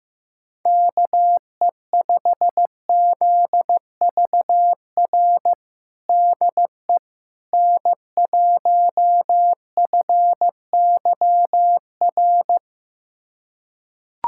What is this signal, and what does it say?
Morse code, heard as a keyed tone, about 15 wpm, 700 Hz. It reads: KE5ZVR DE N1FYR
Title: USA Callsign DE Callsign 15wpm